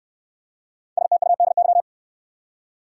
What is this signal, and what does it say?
Morse code, heard as a keyed tone, about 58 wpm, 700 Hz. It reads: STUDY